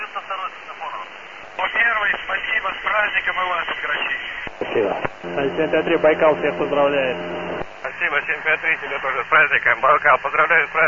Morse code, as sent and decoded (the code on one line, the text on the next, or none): none